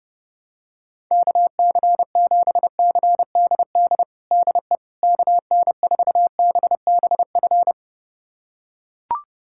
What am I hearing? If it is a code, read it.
Morse code, heard as a keyed tone, about 30 wpm, 700 Hz. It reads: KC7CDD DE KN4BBF
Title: USA Callsign DE Callsign - 6 Characters - 30wpm